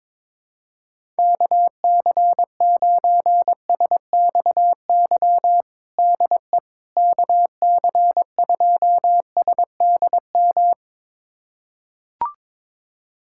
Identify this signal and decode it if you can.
Morse code, heard as a keyed tone, about 22 wpm, 700 Hz. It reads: KC9SXY DE KC2SDM
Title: USA Callsign DE Callsign - 6 Characters - 22wpm